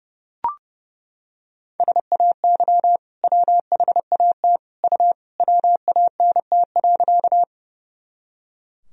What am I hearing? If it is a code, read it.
Morse code, heard as a keyed tone, about 30 wpm, 700 Hz. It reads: SAY WHAT U WANT.
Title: Sentences from Top 200 Words and Abbreviations 30wpm